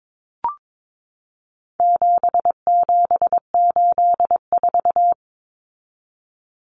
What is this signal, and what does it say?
Morse code, heard as a keyed tone, about 22 wpm, 700 Hz. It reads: 7784
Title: Common Cut Numbers - 4-Digit 22wpm